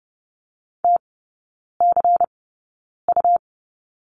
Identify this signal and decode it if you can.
Morse code, heard as a keyed tone, about 30 wpm, spaced as for 9 wpm, 690 Hz. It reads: TCU
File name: Lesson 17e - Letter u - Three Characters 30wpm